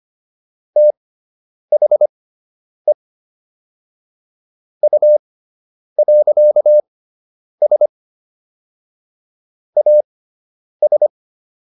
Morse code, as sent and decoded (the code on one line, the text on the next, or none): - .... / ..- .-.-.- ... / .- ...
THE U.S AS